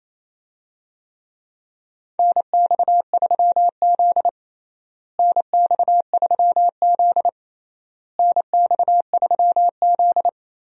Morse code, heard as a keyed tone, about 27 wpm, 700 Hz. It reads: NX3Z NX3Z NX3Z
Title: USA Call Signs - 4 Characters - Repeated 3x - 28wpm